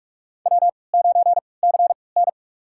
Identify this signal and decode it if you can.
Morse code, heard as a keyed tone, about 46 wpm, spaced as for 25 wpm, 700 Hz. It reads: W9CN
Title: USA Call Signs - 4 Characters - 3x Character Spacing - 45wpm